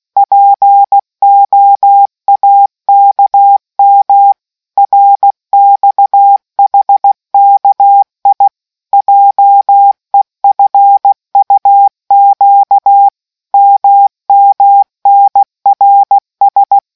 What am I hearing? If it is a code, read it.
Morse code, heard as a keyed tone, about 16 wpm, 800 Hz. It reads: POAKM RXHKI JEFUQ MMNRS